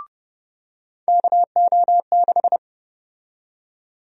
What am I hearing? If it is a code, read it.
Morse code, heard as a keyed tone, about 30 wpm, 700 Hz. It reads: KO6
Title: USA Call Sign Prefixes - 30wpm